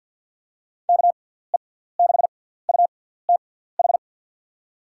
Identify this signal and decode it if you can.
Morse code, heard as a keyed tone, about 51 wpm, spaced as for 17 wpm, 700 Hz. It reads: KE6UTH
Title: USA Call Signs - 6 Characters - 6x Character Spacing - 50wpm